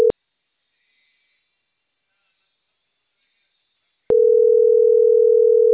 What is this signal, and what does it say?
Ringback tone (North America), 440 + 480 Hz.